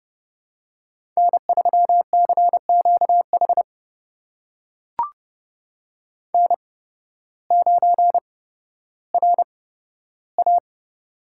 Morse code, heard as a keyed tone, about 30 wpm, spaced as for 14 wpm, 700 Hz. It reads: N3CQH N9RA
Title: USA Call Signs - 8x Character Spacing - 30wpm